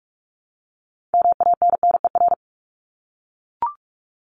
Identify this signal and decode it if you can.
Morse code, heard as a keyed tone, about 44 wpm, 700 Hz. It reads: MANNER
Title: Top 1501-1750 Words 45wpm